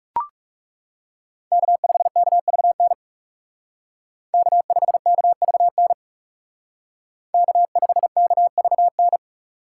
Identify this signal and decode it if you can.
Morse code, heard as a keyed tone, about 36 wpm, 700 Hz. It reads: K5KVN K5KVN K5KVN
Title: Speed Racing - USA Call Signs 30wpm